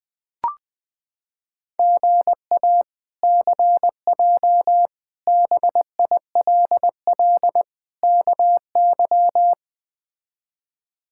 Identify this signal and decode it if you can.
Morse code, heard as a keyed tone, about 20 wpm, 700 Hz. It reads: GA CJ BILL KY
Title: K1USN SST QSO Practice - 20wpm